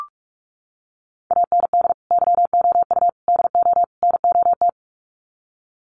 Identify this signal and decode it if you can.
Morse code, heard as a keyed tone, about 44 wpm, 700 Hz. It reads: AND YOU DO NOT
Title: Sets of 4 Words 45wpm